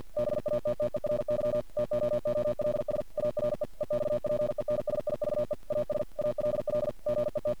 Morse code, heard as a keyed tone, about 50 wpm, 600 Hz. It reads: BATTERY TOOLS ARE EXPENSIVE AS ALL GET
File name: archived / CRSnet